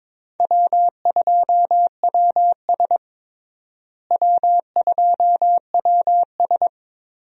Morse code, heard as a keyed tone, about 22 wpm, 700 Hz. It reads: W2WS W2WS
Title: USA Call Signs - 4 Characters - Repeated 3x - 22wpm